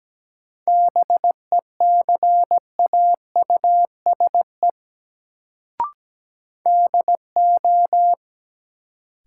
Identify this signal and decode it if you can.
Morse code, heard as a keyed tone, about 17 wpm, 700 Hz. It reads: BECAUSE DO
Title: Sets of 2 Words - New Word Drill - 17wpm